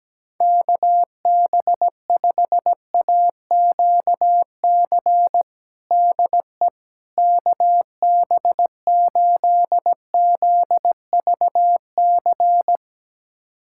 Morse code, heard as a keyed tone, about 17 wpm, 700 Hz. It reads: KB5AQC DE KB8ZVC
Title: USA Callsign DE Callsign - 6 Characters - 17wpm